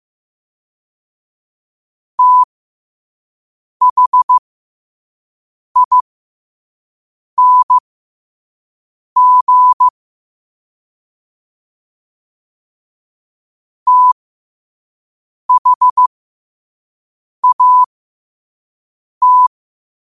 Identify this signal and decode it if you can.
Morse code, heard as a keyed tone, about 15 wpm, spaced as for 5 wpm, 1000 Hz. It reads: THING THAT
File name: qotdfiles / 5wpm